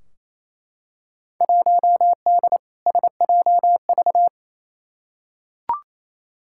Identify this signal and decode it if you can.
Morse code, heard as a keyed tone, about 28 wpm, 700 Hz. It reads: 1D SJV
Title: Field Day Exchanges Encore - 28wpm